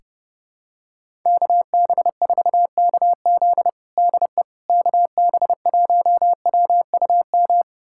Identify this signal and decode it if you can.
Morse code, heard as a keyed tone, about 30 wpm, 700 Hz. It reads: KB4KZ DE KB1WUM
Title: USA Callsign DE Callsign 30wpm